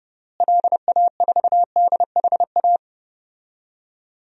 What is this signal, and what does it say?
Morse code, heard as a keyed tone, about 30 wpm, 700 Hz. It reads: LA4DHA
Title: DXCC-100 Call Signs - 6 Characters - Repeated 3x - 30wpm